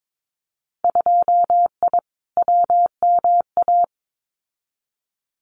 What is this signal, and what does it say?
Morse code, heard as a keyed tone, about 22 wpm, 700 Hz. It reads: 2I WMA
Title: Winter Field Day Exchanges Encore 22wpm